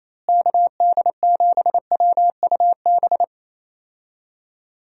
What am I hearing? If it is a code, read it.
Morse code, heard as a keyed tone, about 28 wpm, 700 Hz. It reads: KD7WUB
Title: USA Call Signs - 6 Characters - Repeated 3x - 28wpm